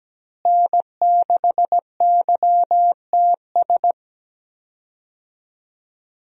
Morse code, heard as a keyed tone, about 17 wpm, 700 Hz. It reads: N6YTS
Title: USA Call Signs - 5 Characters - Repeated 3x - 17wpm